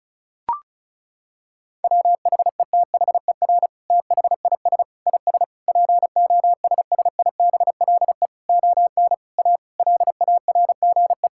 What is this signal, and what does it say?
Morse code, heard as a keyed tone, about 36 wpm, 700 Hz. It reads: WHETHER THIS IS POSSIBLE ON A LARGE
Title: Sentences from Top 1000 Words - Review Part 5 - 35wpm